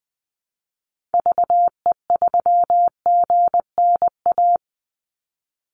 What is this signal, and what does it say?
Morse code, heard as a keyed tone, about 20 wpm, 700 Hz. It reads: VE3GNA
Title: Call Signs - 20wpm